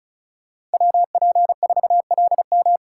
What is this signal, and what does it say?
Morse code, heard as a keyed tone, about 34 wpm, 700 Hz. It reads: WP4LM
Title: USA Call Signs - 5 Characters - 4x Character Spacing - 35wpm